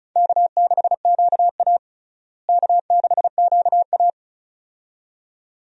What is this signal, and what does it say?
Morse code, heard as a keyed tone, about 36 wpm, 700 Hz. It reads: K6QA K6QA
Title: USA Call Signs - 4 Characters - Repeated 2x - 35wpm